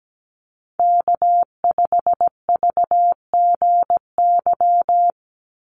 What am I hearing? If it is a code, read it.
Morse code, heard as a keyed tone, about 17 wpm, 700 Hz. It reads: K5VGY
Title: USA Call Signs 17wpm